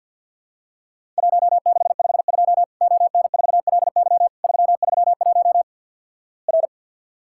Morse code, heard as a keyed tone, about 50 wpm, 700 Hz. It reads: 1652 ON4LY 331 R
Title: Worked All Europe DX Contest - QTC Exchange with Repeats - 50wpm